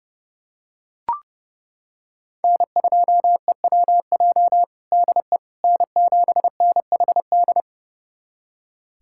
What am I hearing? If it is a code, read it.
Morse code, heard as a keyed tone, about 30 wpm, 700 Hz. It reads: N2EWJ DE N7NHD
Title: USA Callsign DE Callsign - 5 Characters - 30wpm